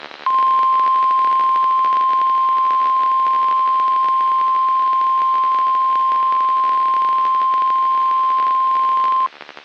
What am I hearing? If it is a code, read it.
Emergency Alert System (SAME) tones (the 1050 Hz attention tone). It is heard from 0.7 to 8.8 s.
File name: eas messages / EAS WXM95 Towanda